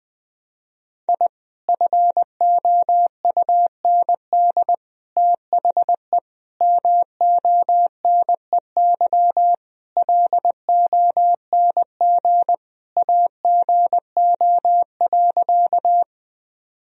Morse code, heard as a keyed tone, about 19 wpm, 700 Hz. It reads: I FOUND THE MONEY LONG AGO.
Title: Sentences from Top 1000 Words - Review Part 3 - 20wpm